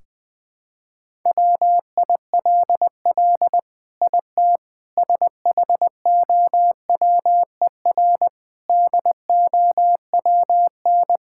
Morse code, heard as a keyed tone, about 21 wpm, 700 Hz. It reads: WILL IT SHOWER DOWN
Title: Sentences from Top 1600 Words 20wpm